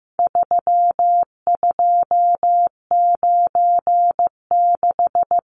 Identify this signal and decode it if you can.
Morse code, heard as a keyed tone, about 15 wpm, 700 Hz. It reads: 3296
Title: Numbers - 4-Digit 15wpm